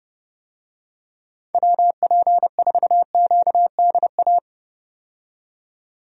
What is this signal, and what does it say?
Morse code, heard as a keyed tone, about 31 wpm, 700 Hz. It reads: WP4QDA
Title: USA Call Signs - 6 Characters - 30wpm